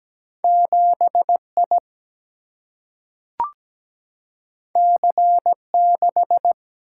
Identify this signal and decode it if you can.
Morse code, heard as a keyed tone, about 17 wpm, 700 Hz. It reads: 7I C6
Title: Single-Letter Single-Number 17wpm